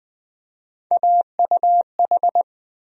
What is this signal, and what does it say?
Morse code, heard as a keyed tone, about 20 wpm, 700 Hz. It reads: AUH
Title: USA Call Sign Postfixes - 3 Letters 20wpm